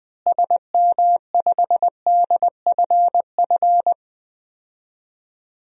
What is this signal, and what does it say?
Morse code, heard as a keyed tone, about 20 wpm, 700 Hz. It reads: SM5DFF
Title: DXCC-100 Call Signs - 6 Characters - Repeated 2x - 20wpm